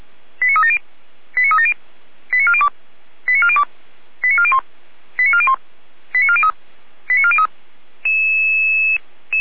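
Five-tone selective call (ZVEI), heard as quick runs of five tones, tones of about 70 ms. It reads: 89370 89370 89402 89402 89401 89401 89403 89403
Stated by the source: ZVEI five-tone selective call received on an SDR.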